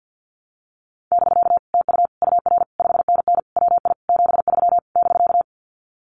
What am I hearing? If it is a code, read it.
Morse code, heard as a keyed tone, about 50 wpm, 700 Hz. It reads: <BK> TU UR 5NN WI 73 <BK>
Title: Common QSO Phrases Encore 50wpm